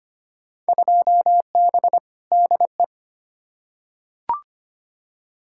Morse code, heard as a keyed tone, about 25 wpm, 700 Hz. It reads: 2B DE
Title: Field Day Exchanges Encore - 25wpm